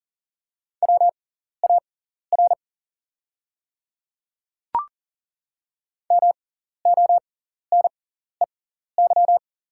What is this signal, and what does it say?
Morse code, heard as a keyed tone, about 40 wpm, spaced as for 14 wpm, 700 Hz. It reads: WAR MONEY
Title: 6x Character Spacing - Top 500 Words - 40wpm